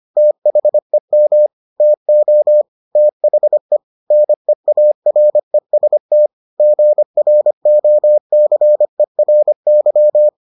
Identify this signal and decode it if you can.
Morse code, heard as a keyed tone, about 25 wpm, 600 Hz. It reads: THEM TO THE NEAREST GROCERY